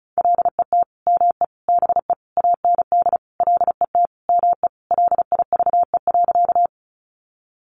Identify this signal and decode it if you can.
Morse code, heard as a keyed tone, about 34 wpm, 700 Hz. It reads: LET ME BE AND LET ME LIVE.
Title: Sentences from Top 300 Words 35wpm